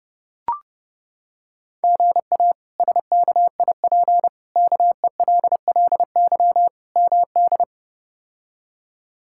Morse code, heard as a keyed tone, about 30 wpm, 700 Hz. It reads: GA SKIP KELLY MD
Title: K1USN SST QSO Practice - 30wpm